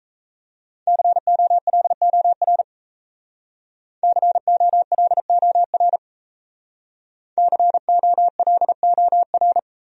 Morse code, heard as a keyed tone, about 36 wpm, 700 Hz. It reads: COLOR COLOR COLOR